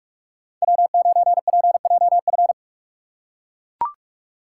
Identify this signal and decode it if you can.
Morse code, heard as a keyed tone, about 43 wpm, 700 Hz. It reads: W9PJF
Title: USA Call Signs - 5 Characters - 6x Character Spacing - 45wpm